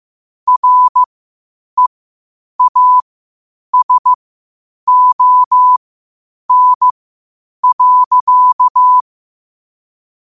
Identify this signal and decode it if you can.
Morse code, heard as a keyed tone, about 15 wpm, spaced as for 8 wpm, 1000 Hz. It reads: REASON.